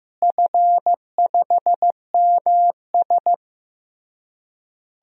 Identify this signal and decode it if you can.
Morse code, heard as a keyed tone, about 15 wpm, 700 Hz. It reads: F5MS
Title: DXCC-100 Call Signs - 4 Characters - Repeated 3x - 15wpm